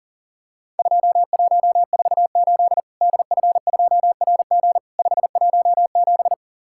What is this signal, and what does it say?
Morse code, heard as a keyed tone, about 40 wpm, 700 Hz. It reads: 2148 DF2RG 517
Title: Worked All Europe DX Contest - QTC Exchange - 40wpm